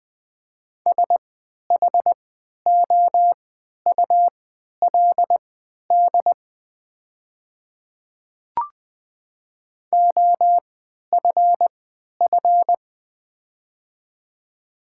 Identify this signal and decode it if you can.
Morse code, heard as a keyed tone, about 20 wpm, spaced as for 11 wpm, 700 Hz. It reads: SHOULD OFF
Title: 3x Character Spacing - Top 500 Words - 20wpm